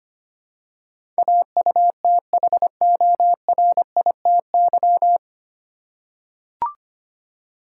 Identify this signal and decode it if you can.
Morse code, heard as a keyed tone, about 25 wpm, 700 Hz. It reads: AUTHORITY
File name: practice / 25wpm / 9 Letter Words - First 3 Letters Given - 25wpm